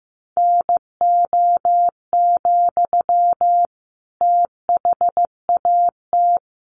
Morse code, heard as a keyed tone, about 15 wpm, 700 Hz. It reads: NO, THAT
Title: Sentences from Top 600 Words 15wpm